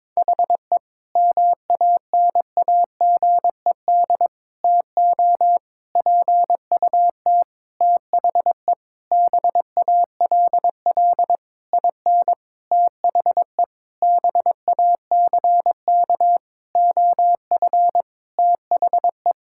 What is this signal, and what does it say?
Morse code, heard as a keyed tone, about 22 wpm, 700 Hz. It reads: HE MANAGED TO PUT THE BALL IN THE BACK OF THE
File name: Sentences with 9-Word Phrases - 22wpm